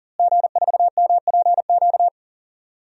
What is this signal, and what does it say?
Morse code, heard as a keyed tone, about 40 wpm, 700 Hz. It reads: G4MPQ
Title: DXCC-100 Call Signs - 5 Characters - Repeated 3x - 40wpm